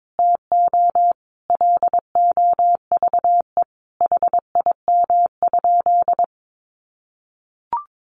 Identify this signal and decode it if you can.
Morse code, heard as a keyed tone, about 22 wpm, 700 Hz. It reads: TO LOVE HIM?
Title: Sentences from Top 1100 Words 22wpm